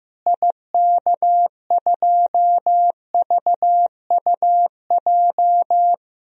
Morse code, heard as a keyed tone, about 15 wpm, 700 Hz. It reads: IK2VUJ
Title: DXCC-100 Call Signs - 6 Characters - Repeated 3x - 15wpm